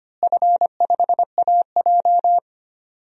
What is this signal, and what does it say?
Morse code, heard as a keyed tone, about 25 wpm, 700 Hz. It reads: F5AJ
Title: DXCC-100 Call Signs - 4 Characters - Repeated 3x - 25wpm